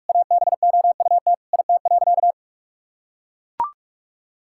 Morse code, heard as a keyed tone, about 44 wpm, 700 Hz. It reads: ABOUT IT.
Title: Sentences from Top 1000 Words - Review Part 6 - 45wpm